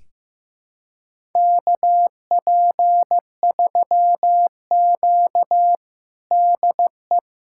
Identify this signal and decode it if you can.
Morse code, heard as a keyed tone, about 15 wpm, 700 Hz. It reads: KP3Q DE
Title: USA Callsign DE Callsign - 4 Characters - 15wpm